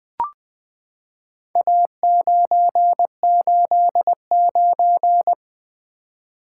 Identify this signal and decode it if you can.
Morse code, heard as a keyed tone, about 20 wpm, 700 Hz. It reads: A989
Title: Four Letter-Number Combinations - 20wpm